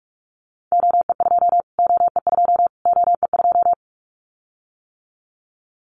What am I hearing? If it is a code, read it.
Morse code, heard as a keyed tone, about 44 wpm, 700 Hz. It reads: OE2 OE2 OE2
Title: Callsign Prefixes for Top 25 DXCC - Repeated 3x - 45wpm